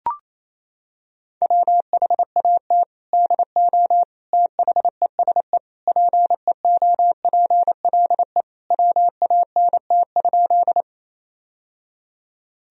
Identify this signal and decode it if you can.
Morse code, heard as a keyed tone, about 28 wpm, 700 Hz. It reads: WHAT DO THESE PEOPLE WANT?
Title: Sentences from Top 1000 Words - Review Part 1 - 28wpm